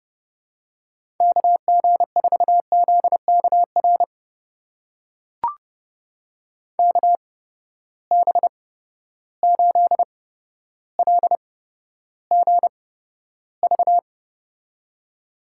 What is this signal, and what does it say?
Morse code, heard as a keyed tone, about 30 wpm, spaced as for 13 wpm, 700 Hz. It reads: KG4ZKR KB8LGV
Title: USA Call Signs - 6 Characters - 8x Character Spacing - 30wpm